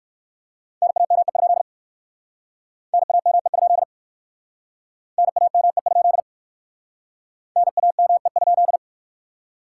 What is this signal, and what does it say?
Morse code, heard as a keyed tone, about 60 wpm, 700 Hz. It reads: NAME? NAME? NAME? NAME?